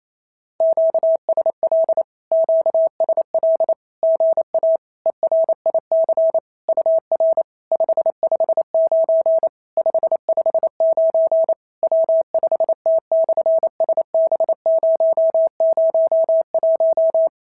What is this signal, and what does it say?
Morse code, heard as a keyed tone, about 28 wpm, 650 Hz. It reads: QSL QSL GA ERIC UR 559 559 W5T/SB001
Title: SOTA QSO Practice 28wpm